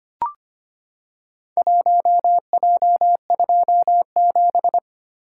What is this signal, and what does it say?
Morse code, heard as a keyed tone, about 25 wpm, 700 Hz. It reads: 1J27
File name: practice / 25wpm / Four Letter-Number Combinations - 25wpm